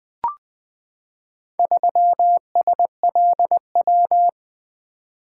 Morse code, heard as a keyed tone, about 20 wpm, 700 Hz. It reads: 3SLW
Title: Four Letter-Number Combinations - 20wpm